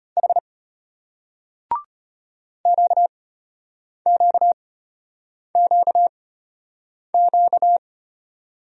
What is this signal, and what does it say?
Morse code, heard as a keyed tone, about 33 wpm, spaced as for 8 wpm, 700 Hz. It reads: H QQQQ